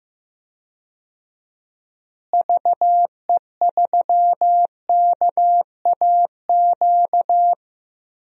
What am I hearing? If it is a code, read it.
Morse code, heard as a keyed tone, about 15 wpm, 700 Hz. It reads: VE3KAQ
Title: DXCC-100 Call Signs - 6 Characters - Repeated 2x - 15wpm